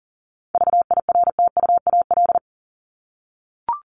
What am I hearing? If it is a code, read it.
Morse code, heard as a keyed tone, about 40 wpm, 700 Hz. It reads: VIRTUAL
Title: LY Suffix - Top 100 Words with Base Words 40wpm